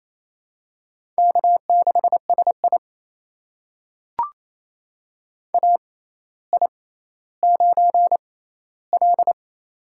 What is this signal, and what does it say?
Morse code, heard as a keyed tone, about 28 wpm, spaced as for 14 wpm, 700 Hz. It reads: K6SI AI9L
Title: USA Call Signs - 4 Characters - 6x Character Spacing - 28wpm